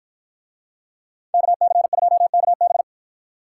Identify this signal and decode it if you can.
Morse code, heard as a keyed tone, about 53 wpm, 700 Hz. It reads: KC2KB